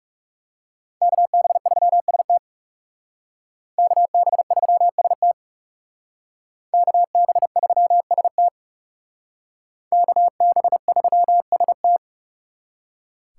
Morse code, heard as a keyed tone, about 40 wpm, 700 Hz. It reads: KB3ST KB3ST KB3ST KB3ST